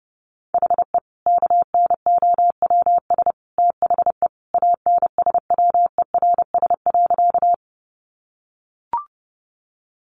Morse code, heard as a keyed tone, about 30 wpm, 700 Hz. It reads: HE KNOWS THE ANSWERS.
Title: Additional Easy Phrases 30wpm